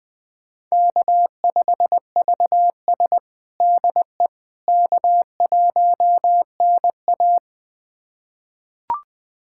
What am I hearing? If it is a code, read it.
Morse code, heard as a keyed tone, about 20 wpm, 700 Hz. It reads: K5VS DE K1NA
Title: USA Callsign DE Callsign - 4 Characters - 20wpm